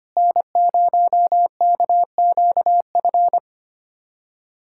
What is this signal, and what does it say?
Morse code, heard as a keyed tone, about 25 wpm, 700 Hz. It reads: N0KQF
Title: USA Callsign DE Callsign - 5 Characters - 25wpm